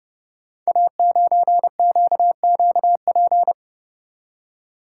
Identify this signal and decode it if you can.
Morse code, heard as a keyed tone, about 28 wpm, 700 Hz. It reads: A9QQP
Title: Code Groups - 5 Letter:Numbers - 30wpm